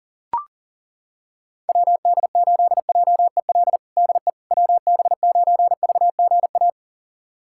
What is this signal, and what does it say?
Morse code, heard as a keyed tone, about 37 wpm, 700 Hz. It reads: WD8JEL DE WB9VGA
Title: USA Callsign DE Callsign - 6 Characters - 40wpm